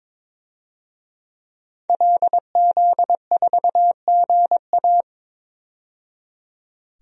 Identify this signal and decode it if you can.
Morse code, heard as a keyed tone, about 22 wpm, 700 Hz. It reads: LZ4GA
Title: DXCC-100 Call Signs - 5 Characters - 22wpm